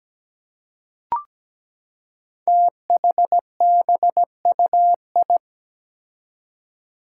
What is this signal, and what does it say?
Morse code, heard as a keyed tone, about 17 wpm, 700 Hz. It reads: THBUI
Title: Five Challenging Letter-Number Combinations 17wpm